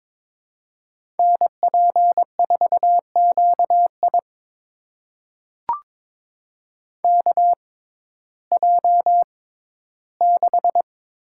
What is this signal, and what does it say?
Morse code, heard as a keyed tone, about 22 wpm, spaced as for 14 wpm, 700 Hz. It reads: NP4QI KJ6
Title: USA Call Signs - 5 Characters - 6x Character Spacing - 22wpm